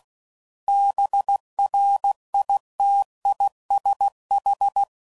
Morse code, heard as a keyed tone, about 16 wpm, 800 Hz. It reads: BRITISH